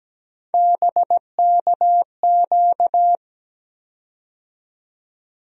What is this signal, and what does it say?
Morse code, heard as a keyed tone, about 17 wpm, 700 Hz. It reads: BKQ
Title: USA Call Sign Postfixes - 3 Letters 17wpm